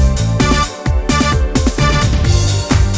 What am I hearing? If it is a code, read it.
There is no code.